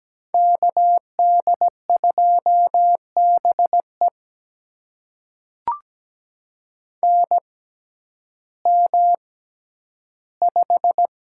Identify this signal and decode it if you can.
Morse code, heard as a keyed tone, about 17 wpm, spaced as for 10 wpm, 700 Hz. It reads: KD2BE NM5
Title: USA Call Signs - 6x Character Spacing - 17wpm